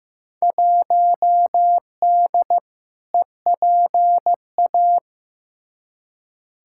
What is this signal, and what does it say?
Morse code, heard as a keyed tone, about 15 wpm, 700 Hz. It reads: 1D EPA
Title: Field Day Exchanges - 15wpm